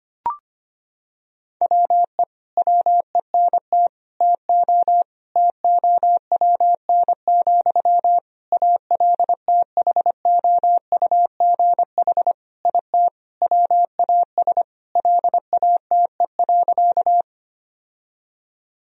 Morse code, heard as a keyed tone, about 25 wpm, 700 Hz. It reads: WE WENT TO TOWN, ALTHOUGH IT WAS LATE.
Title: Sentences from Top 1000 Words - Review Part 3 - 25wpm